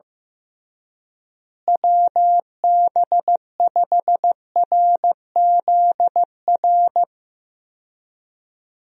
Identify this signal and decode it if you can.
Morse code, heard as a keyed tone, about 15 wpm, 700 Hz. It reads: WB5RZR
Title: USA Call Signs - 6 Characters - Repeated 3x - 15wpm